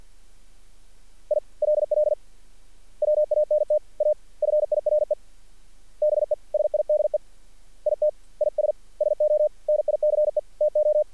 Morse code, heard as a keyed tone, about 48 wpm, 600 Hz. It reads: ILL WANT A FIRE BESIDE IT IS SO NICE TO
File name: archived / CRSnet